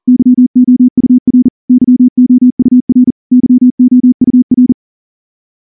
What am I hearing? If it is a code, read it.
Morse code, heard as a keyed tone, about 38 wpm, 250 Hz. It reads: YOUR YOUR YOUR
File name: morse-training-40WPM